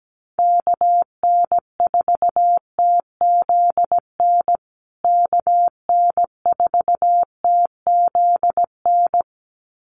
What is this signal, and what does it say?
Morse code, heard as a keyed tone, about 17 wpm, 700 Hz. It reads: KN4TZN KN4TZN
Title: USA Call Signs Repeated 2x 17wpm